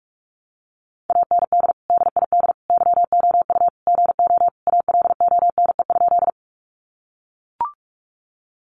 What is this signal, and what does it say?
Morse code, heard as a keyed tone, about 44 wpm, 700 Hz. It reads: AND DID YOU GO ALONE?
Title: Sentences from Top 1000 Words 45wpm